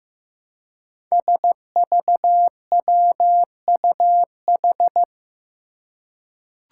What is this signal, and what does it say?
Morse code, heard as a keyed tone, about 15 wpm, 700 Hz. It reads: SVWUH
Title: Five Challenging Letter-Number Combinations 15wpm